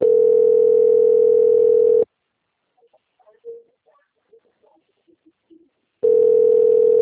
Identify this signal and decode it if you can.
Ringback tone (North America), 440 + 480 Hz.